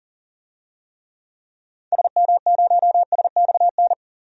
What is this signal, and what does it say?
Morse code, heard as a keyed tone, about 40 wpm, 700 Hz. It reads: SM0SXN
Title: DXCC-100 Call Signs - 6 Characters - 40wpm